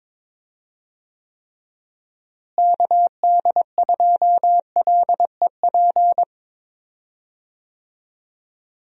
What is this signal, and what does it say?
Morse code, heard as a keyed tone, about 22 wpm, 700 Hz. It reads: KD2LEP
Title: USA Call Signs - 6 Characters - Repeated 3x - 22wpm